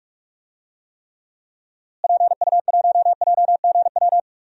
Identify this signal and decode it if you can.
Morse code, heard as a keyed tone, about 47 wpm, 700 Hz. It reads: PU1JGW
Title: DXCC-100 Call Signs - 6 Characters - Repeated 2x - 45wpm